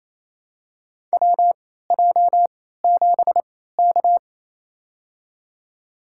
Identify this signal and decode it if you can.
Morse code, heard as a keyed tone, about 27 wpm, spaced as for 16 wpm, 700 Hz. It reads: WJ7K
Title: USA Call Signs - 4 Characters - 3x Character Spacing - 28wpm